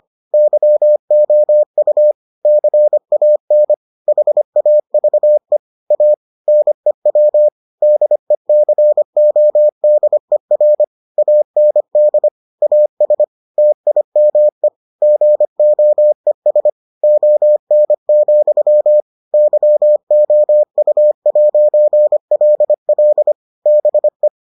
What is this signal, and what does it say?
Morse code, heard as a keyed tone, about 25 wpm, 600 Hz. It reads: YOU CAN HAVE A NEW DECODER AND AS TIME GOES ON, YOU'LL BE